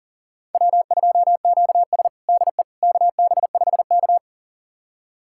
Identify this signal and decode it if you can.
Morse code, heard as a keyed tone, about 40 wpm, 700 Hz. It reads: W2QS DE KB5K
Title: USA Callsign DE Callsign - 4 Characters - 40wpm